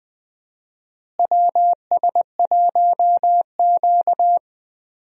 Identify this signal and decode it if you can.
Morse code, heard as a keyed tone, about 20 wpm, 700 Hz. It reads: WS1Q
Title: USA Call Signs - 4 Characters - Repeated 3x - 20wpm